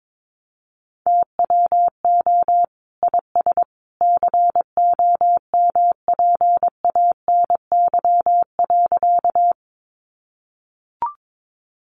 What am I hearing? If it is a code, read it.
Morse code, heard as a keyed tone, about 22 wpm, 700 Hz. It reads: TWO IS COMPANY.
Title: Sentences from Top 600 Words 22wpm